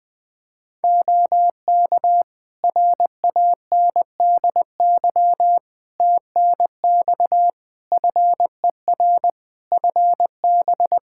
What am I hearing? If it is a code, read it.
Morse code, heard as a keyed tone, about 20 wpm, 700 Hz. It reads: OK RANDY TNX FER FB
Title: Practice QSO - Exchange 4 - Ending - 20wpm